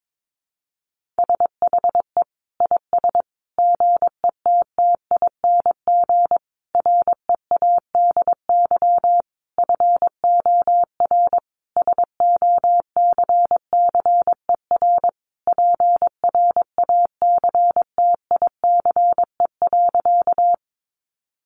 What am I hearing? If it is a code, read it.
Morse code, heard as a keyed tone, about 22 wpm, 700 Hz. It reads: SHE IS GETTING READY FOR SOCCER PRACTICE.
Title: Easy Phrases 22wpm